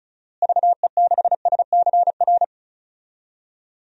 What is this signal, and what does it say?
Morse code, heard as a keyed tone, about 36 wpm, 700 Hz. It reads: VE6SCR
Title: DXCC-100 Call Signs - 6 Characters - Repeated 3x - 35wpm